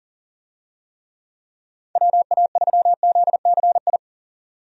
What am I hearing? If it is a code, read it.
Morse code, heard as a keyed tone, about 40 wpm, 700 Hz. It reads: WA3ZCI
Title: USA Call Signs - 6 Characters - 40wpm